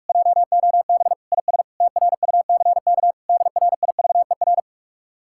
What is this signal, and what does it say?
Morse code, heard as a keyed tone, about 49 wpm, 700 Hz. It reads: JOB IS TRUCK DRIVER